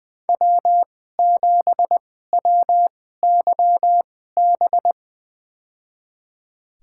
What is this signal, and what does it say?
Morse code, heard as a keyed tone, about 20 wpm, spaced as for 14 wpm, 700 Hz. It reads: W7WYB
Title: USA Call Signs - 5 Characters - 2x Character Spacing - 20wpm